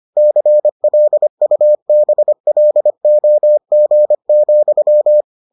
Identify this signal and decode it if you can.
Morse code, heard as a keyed tone, about 25 wpm, 600 Hz. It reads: CLUBLOG,